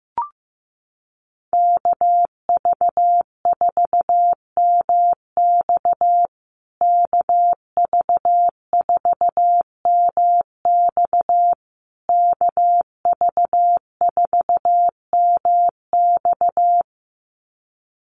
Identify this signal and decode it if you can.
Morse code, heard as a keyed tone, about 14 wpm, 700 Hz. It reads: KV4MX KV4MX KV4MX
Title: USA Call Signs Repeated 3x 15wpm